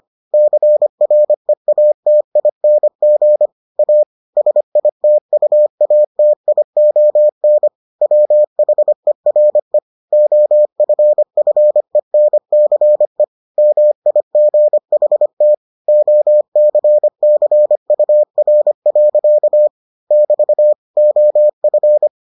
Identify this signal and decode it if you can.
Morse code, heard as a keyed tone, about 25 wpm, 600 Hz. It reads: CREATING A SITUATION WHERE OFFENCE MIGHT OCCUR. <BT> OF